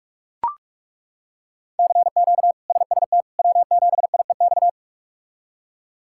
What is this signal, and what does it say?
Morse code, heard as a keyed tone, about 41 wpm, 700 Hz. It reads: CQ SST W7IEX
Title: K1USN SST QSO Practice - 45wpm